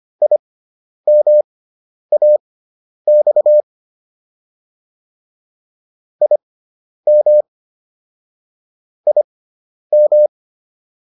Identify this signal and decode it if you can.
Morse code, heard as a keyed tone, about 25 wpm, spaced as for 10 wpm, 600 Hz. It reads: IMAX IM IM
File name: IM4 10